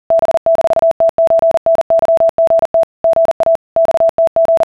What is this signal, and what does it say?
Morse code (Russian alphabet), heard as a keyed tone, about 40 wpm, 660 Hz. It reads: Б-ТЧНЫГТ ГА ЬТГ